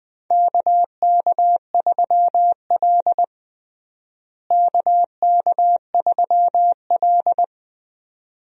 Morse code, heard as a keyed tone, about 21 wpm, 700 Hz. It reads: KK3L KK3L
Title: USA Call Signs - 4 Characters - Repeated 2x - 20wpm